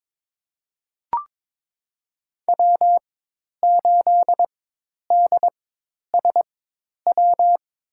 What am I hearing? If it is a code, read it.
Morse code, heard as a keyed tone, about 22 wpm, spaced as for 10 wpm, 700 Hz. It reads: W8DSW
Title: USA Call Signs - 4x Character Spacing - 22wpm